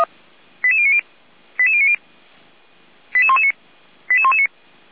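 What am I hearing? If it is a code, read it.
Five-tone selective call (ZVEI), heard as quick runs of five tones, tones of about 70 ms. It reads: 88090 88090 88119 88119
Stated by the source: Received by SDR; ZVEI five-tone code 88125.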